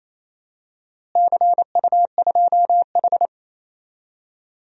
Morse code, heard as a keyed tone, about 27 wpm, 700 Hz. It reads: CU2H
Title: DXCC-100 Call Signs - 4 Characters - Repeated 3x - 28wpm